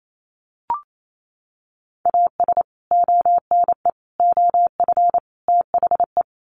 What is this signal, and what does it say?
Morse code, heard as a keyed tone, about 28 wpm, 700 Hz. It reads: AS ONE OF THE
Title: Sets of 4 Words 28wpm